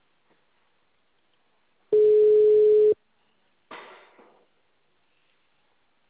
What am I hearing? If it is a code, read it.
Ringback tone (Europe, 425 Hz).